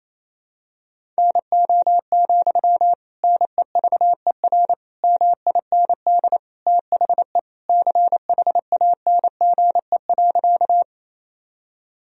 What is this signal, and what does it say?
Morse code, heard as a keyed tone, about 28 wpm, 700 Hz. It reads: NO, NEVER MIND THE CHANGE.
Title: Sentences from Top 1000 Words - Review Part 1 - 28wpm